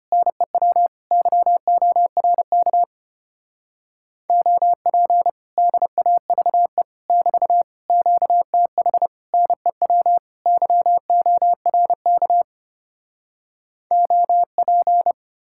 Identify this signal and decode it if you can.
Morse code, heard as a keyed tone, about 33 wpm, 700 Hz. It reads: NEW YORK OP DAVE <BT> QTH NEW YORK OP